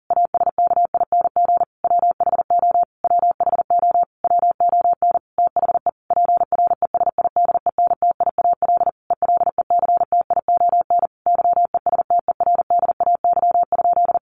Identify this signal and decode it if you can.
Morse code, heard as a keyed tone, about 37 wpm, 700 Hz. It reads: ASKING WHO WHO WON THE PRESIDENTIAL ELECTION YESTERDAY?
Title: A Sound Of Thunder 40wpm